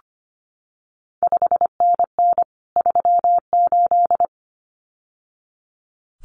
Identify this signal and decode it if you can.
Morse code, heard as a keyed tone, about 25 wpm, 700 Hz. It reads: 5NN 38
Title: CQ WW DX Contest Exchange Encore - 25wpm